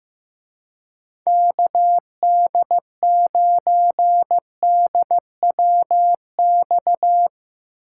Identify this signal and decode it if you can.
Morse code, heard as a keyed tone, about 16 wpm, 700 Hz. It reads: KD9DWX
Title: Speed Racing - USA 2x3 Call Signs 15wpm